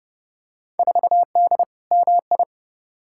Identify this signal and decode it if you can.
Morse code, heard as a keyed tone, about 29 wpm, 700 Hz. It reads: 4D MI
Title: Field Day Exchanges Encore - 30wpm